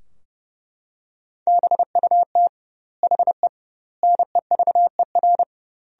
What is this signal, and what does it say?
Morse code, heard as a keyed tone, about 30 wpm, 700 Hz. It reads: BUT HE NEVER
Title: 2x Word-Spacing - Sentences from Top 400 Words - 30wpm